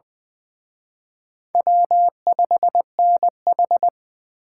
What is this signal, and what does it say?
Morse code, heard as a keyed tone, about 24 wpm, 700 Hz. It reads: W5NH
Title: Speed Racing - USA 1x2 Call Signs 20wpm